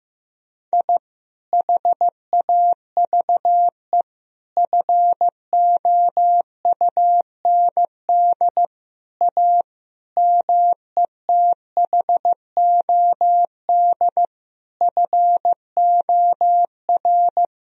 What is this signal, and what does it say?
Morse code, heard as a keyed tone, about 16 wpm, 700 Hz. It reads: I HAVE FOUND A METHOD FOR
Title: Sentences from Top 1000 Words - Review Part 3 - 15wpm